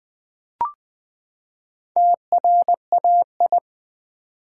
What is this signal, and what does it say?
Morse code, heard as a keyed tone, about 20 wpm, 700 Hz. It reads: TRAI
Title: Build A Word - 4 Letters At A Time - 8-Letter Words - 20wpm